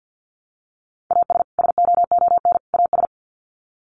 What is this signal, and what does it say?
Morse code, heard as a keyed tone, about 50 wpm, 700 Hz. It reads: AS SOON AS
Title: Sets of 3 Words 50wpm